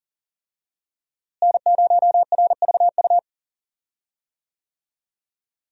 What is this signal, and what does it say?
Morse code, heard as a keyed tone, about 38 wpm, 700 Hz. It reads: N0RVU
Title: USA Call Signs - 5 Characters - Repeated 3x - 40wpm